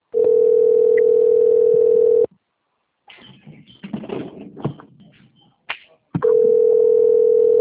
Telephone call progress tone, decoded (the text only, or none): ringback tone